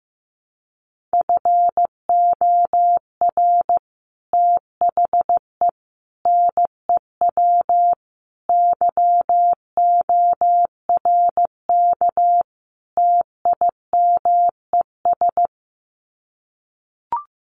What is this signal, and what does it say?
Morse code, heard as a keyed tone, about 15 wpm, 700 Hz. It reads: FOR THE NEW YORK TIMES
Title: Sets of 5 Words 15wpm